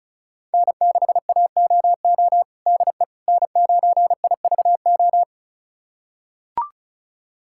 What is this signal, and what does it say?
Morse code, heard as a keyed tone, about 36 wpm, 700 Hz. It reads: N6AOO DE N9IVO
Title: USA Callsign DE Callsign - 5 Characters - 35wpm